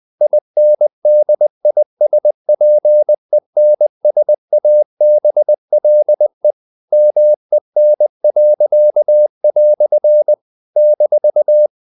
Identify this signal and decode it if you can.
Morse code, heard as a keyed tone, about 20 wpm, 600 Hz. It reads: INDISPENSABLE MEN." -
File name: CW / Practice / EnglishQuotations0320